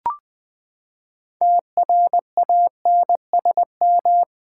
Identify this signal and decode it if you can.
Morse code, heard as a keyed tone, about 20 wpm, 700 Hz. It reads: TRANSM
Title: Build A Word - 6 Letters At A Time - 12-Letter Words - 20wpm